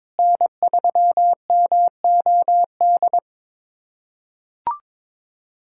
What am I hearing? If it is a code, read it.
Morse code, heard as a keyed tone, about 22 wpm, 700 Hz. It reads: N3MOD
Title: USA Callsign DE Callsign - 5 Characters - 22wpm